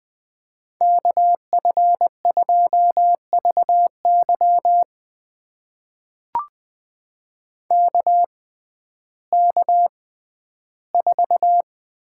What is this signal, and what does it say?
Morse code, heard as a keyed tone, about 20 wpm, spaced as for 12 wpm, 700 Hz. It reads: KF2VY KK4
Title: USA Call Signs - 6x Character Spacing - 20wpm